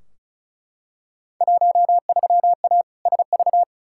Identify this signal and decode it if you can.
Morse code, heard as a keyed tone, about 34 wpm, 700 Hz. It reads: 13A SV
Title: Field Day Exchanges Encore - 35wpm